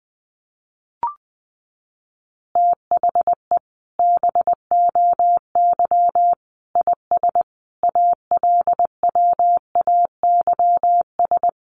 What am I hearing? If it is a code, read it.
Morse code, heard as a keyed tone, about 20 wpm, 700 Hz. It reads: THE BOY IS ALWAYS
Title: Additional Easy Phrases 20wpm